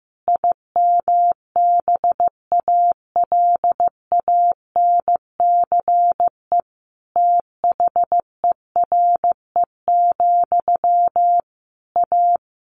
Morse code, heard as a keyed tone, about 15 wpm, 700 Hz. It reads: IMBALANCE THERE, A
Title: A Sound Of Thunder 15wpm